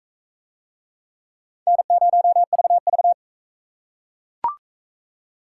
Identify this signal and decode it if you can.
Morse code, heard as a keyed tone, about 39 wpm, 700 Hz. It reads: N0VV